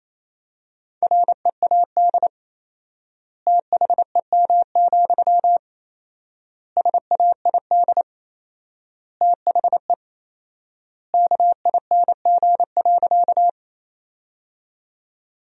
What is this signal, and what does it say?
Morse code, heard as a keyed tone, about 28 wpm, 700 Hz. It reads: READ THEM, SAID THE KING.
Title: 4x Word-Spacing - Sentences from Top 400 Words - 28wpm